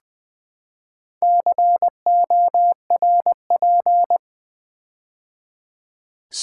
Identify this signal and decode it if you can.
Morse code, heard as a keyed tone, about 20 wpm, 700 Hz. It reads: CORP
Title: Build A Word - 4 Letters At A Time - 11-Letter Words - 20wpm